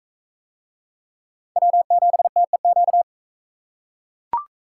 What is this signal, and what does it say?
Morse code, heard as a keyed tone, about 40 wpm, 700 Hz. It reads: W7TEQ